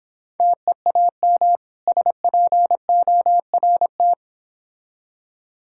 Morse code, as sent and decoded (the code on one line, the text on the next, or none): - . .- -- / ... .--. --- .-. -
TEAM SPORT